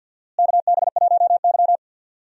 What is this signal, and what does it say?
Morse code, heard as a keyed tone, about 53 wpm, 700 Hz. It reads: KB1Y
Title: USA Call Signs - 4 Characters - Repeated 3x - 50wpm